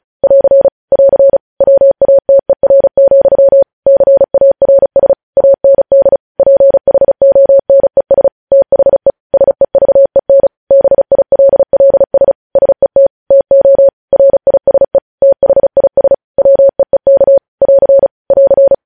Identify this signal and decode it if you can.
Morse code, heard as a keyed tone, about 36 wpm, 550 Hz. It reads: <AR> <AR> WATER, CARS AND PHONES THE SEVEN BILLS SET TO RISE THIS WEEK <AR> <AR>
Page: News headlines in Morse code at 35 wpm